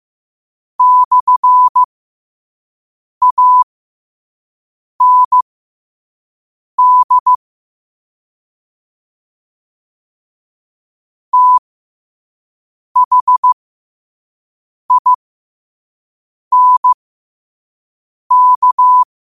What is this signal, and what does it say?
Morse code, heard as a keyed tone, about 15 wpm, spaced as for 5 wpm, 1000 Hz. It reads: /AND THINK
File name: qotdfiles / 5wpm